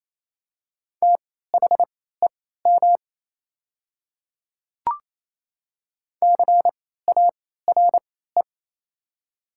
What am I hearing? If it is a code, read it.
Morse code, heard as a keyed tone, about 28 wpm, spaced as for 16 wpm, 700 Hz. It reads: THEM CARE
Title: 3x Character Spacing - Top 300 Words - 28wpm